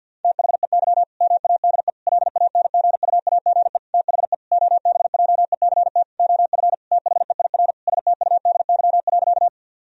Morse code, heard as a keyed tone, about 50 wpm, 700 Hz. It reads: THEY MADE LANGUAGE THE OBJECT OF THEIR STUDY.
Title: Sentences from Top 1000 Words - Review Part 3 - 50wpm